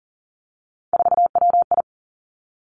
Morse code, heard as a keyed tone, about 41 wpm, 700 Hz. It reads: 4WI
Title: Three Challenging Letter-Number Combinations 40wpm